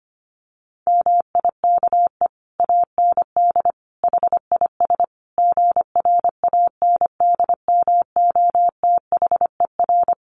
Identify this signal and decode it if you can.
Morse code, heard as a keyed tone, about 25 wpm, 700 Hz. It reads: MIKE AND HIS GRANDMOTHER
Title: Additional Easy Phrases 25wpm